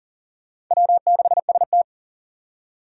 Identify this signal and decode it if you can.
Morse code, heard as a keyed tone, about 40 wpm, 700 Hz. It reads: W6ST
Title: USA Call Signs - 4x Character Spacing - 40wpm